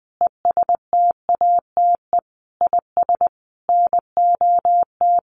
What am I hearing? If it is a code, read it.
Morse code, heard as a keyed tone, about 20 wpm, 700 Hz. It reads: ESTATE IS NOT
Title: Sentences from Top 1200 Words 20wpm